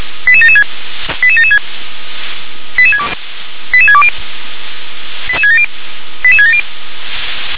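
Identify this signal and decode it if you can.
Five-tone selective call (ZVEI), heard as quick runs of five tones, tones of about 70 ms. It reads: 88776 88776 88620 88670 88670
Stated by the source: ZVEI five-tone selective call 88774, received by SDR.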